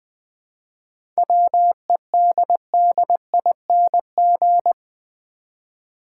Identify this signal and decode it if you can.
Morse code, heard as a keyed tone, about 20 wpm, 700 Hz. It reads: WEDDING
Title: 7 Letter Words - First 3 Letters Given - 20wpm